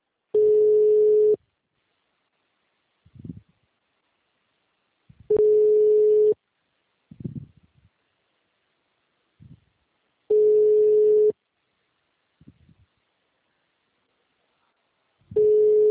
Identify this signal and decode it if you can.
Ringback tone (Europe, 425 Hz).
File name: out- -2011- - - .70329